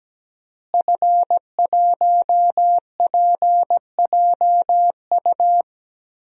Morse code, heard as a keyed tone, about 17 wpm, 700 Hz. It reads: F1PJU
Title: DXCC-100 Call Signs - 5 Characters - Repeated 2x - 17wpm